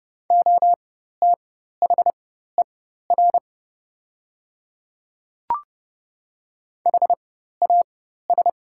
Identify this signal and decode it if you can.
Morse code, heard as a keyed tone, about 30 wpm, spaced as for 14 wpm, 700 Hz. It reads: OTHER HAS
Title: 4x Character Spacing - Top 100 Words - 30wpm